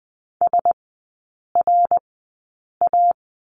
Morse code, heard as a keyed tone, about 20 wpm, spaced as for 8 wpm, 700 Hz. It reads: SRA
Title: 053 Lesson 10e - Letter r - Three Characters 20wpm